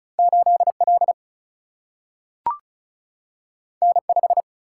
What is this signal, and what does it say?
Morse code, heard as a keyed tone, about 36 wpm, 700 Hz. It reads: 8L N5
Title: Single-Letter Single-Number 35wpm